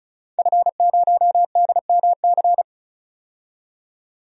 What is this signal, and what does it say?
Morse code, heard as a keyed tone, about 36 wpm, 700 Hz. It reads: F0DMC
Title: DXCC-100 Call Signs - 5 Characters - 35wpm